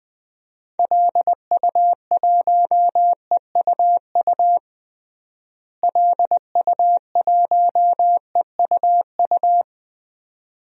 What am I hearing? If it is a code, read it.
Morse code, heard as a keyed tone, about 20 wpm, 700 Hz. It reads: LU1EUU LU1EUU
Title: DXCC-100 Call Signs - 6 Characters - Repeated 2x - 20wpm